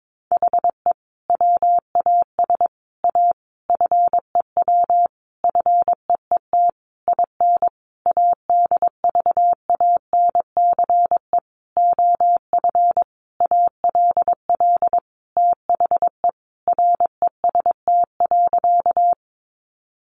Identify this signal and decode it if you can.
Morse code, heard as a keyed tone, about 22 wpm, 700 Hz. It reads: HE WAS A FEW FEET IN ADVANCE OF ALL THE REST.
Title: Sentences from Top 1400 Words 22wpm